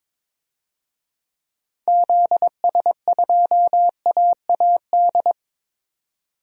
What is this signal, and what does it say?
Morse code, heard as a keyed tone, about 22 wpm, 700 Hz. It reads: ZS2AAD
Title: DXCC-100 Call Signs - 6 Characters - 22wpm